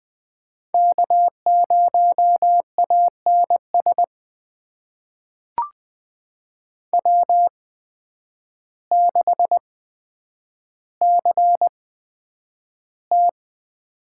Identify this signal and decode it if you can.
Morse code, heard as a keyed tone, about 20 wpm, spaced as for 9 wpm, 700 Hz. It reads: K0ANS W6CT
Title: USA Call Signs - 5 Characters - 8x Character Spacing - 20wpm